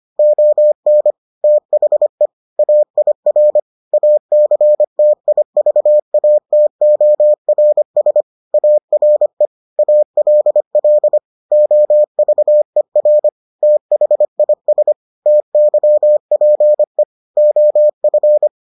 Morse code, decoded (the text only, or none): ON THE AIR ACTIVATORS ARE ALL OVER THIS TYPE OF